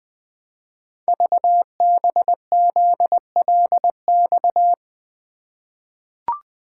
Morse code, heard as a keyed tone, about 20 wpm, 700 Hz. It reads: VBZLX
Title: Code Groups - 5 Letters - 20wpm